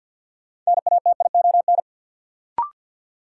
Morse code, heard as a keyed tone, about 49 wpm, 700 Hz. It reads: NATION
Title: Sentences from Top 900 Words - New Word Drill - 50wpm